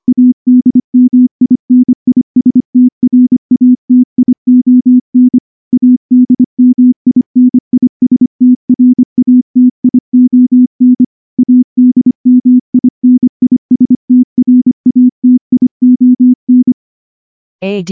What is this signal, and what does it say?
Morse code, heard as a keyed tone, about 24 wpm, 250 Hz. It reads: ADMINISTRATION ADMINISTRATION ADMINISTRATION
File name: morse-training-25WPM